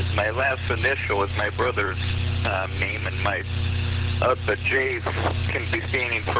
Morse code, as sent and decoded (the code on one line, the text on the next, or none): none